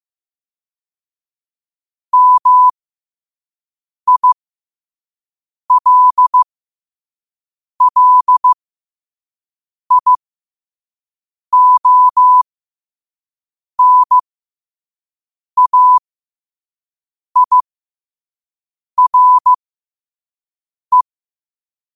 Morse code, heard as a keyed tone, about 15 wpm, spaced as for 5 wpm, 1000 Hz. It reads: MILLIONAIRE
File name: qotdfiles / 5wpm